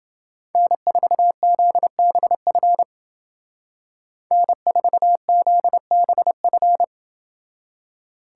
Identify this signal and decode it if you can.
Morse code, heard as a keyed tone, about 28 wpm, 700 Hz. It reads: N4ZBF N4ZBF